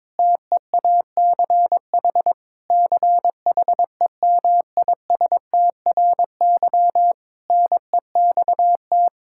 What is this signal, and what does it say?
Morse code, heard as a keyed tone, about 21 wpm, 700 Hz. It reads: TEACH CHEMISTRY NEXT
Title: Sentences from Top 1000 Words - Review Part 6 - 22wpm